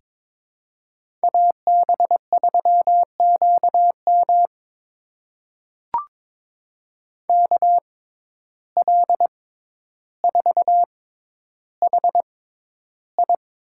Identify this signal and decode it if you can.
Morse code, heard as a keyed tone, about 22 wpm, spaced as for 11 wpm, 700 Hz. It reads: AB3QM KL4HI
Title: USA Call Signs - 6x Character Spacing - 22wpm